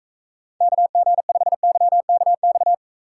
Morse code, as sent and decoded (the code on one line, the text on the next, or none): -.- --. ..... -.-- -.- -..-
KG5YKX